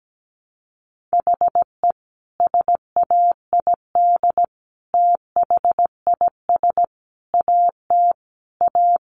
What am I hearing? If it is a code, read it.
Morse code, heard as a keyed tone, about 17 wpm, 700 Hz. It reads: HE SAID THIS AT A
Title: Sentences from Top 1000 Words 17wpm